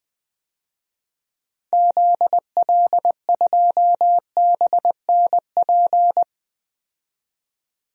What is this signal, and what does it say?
Morse code, heard as a keyed tone, about 20 wpm, 700 Hz. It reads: ZL2BNP
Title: DXCC-100 Call Signs - 6 Characters - 20wpm